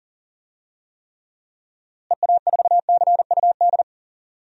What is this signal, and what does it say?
Morse code, heard as a keyed tone, about 40 wpm, 700 Hz. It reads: EA4CUD
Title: DXCC-100 Call Signs - 6 Characters - Repeated 2x - 40wpm